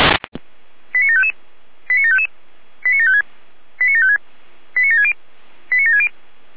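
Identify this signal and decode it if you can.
Five-tone selective call (ZVEI), heard as quick runs of five tones, tones of about 70 ms. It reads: 89755 89755 89756 89756 89760 89760
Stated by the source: ZVEI five-tone selective call received with an SDR.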